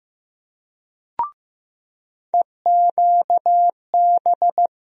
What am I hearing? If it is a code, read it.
Morse code, heard as a keyed tone, about 15 wpm, 700 Hz. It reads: EQB
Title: USA Call Sign Postfixes - 3 Letters 15wpm